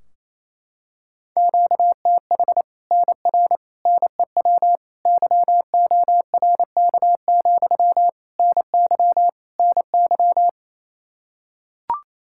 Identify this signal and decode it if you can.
Morse code, heard as a keyed tone, about 27 wpm, 700 Hz. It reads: QTH NR NEW YORK, NY NY
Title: Practice QSO - Exchange 1 - Beginning - 28wpm